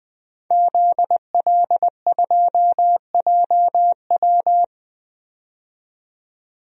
Morse code, heard as a keyed tone, about 20 wpm, 700 Hz. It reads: ZL2JW
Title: DXCC-100 Call Signs - 5 Characters - 20wpm